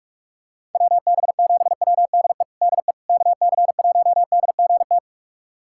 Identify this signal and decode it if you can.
Morse code, heard as a keyed tone, about 47 wpm, 700 Hz. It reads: WB7WDE DE KC1DGT
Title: USA Callsign DE Callsign - 6 Characters - 45wpm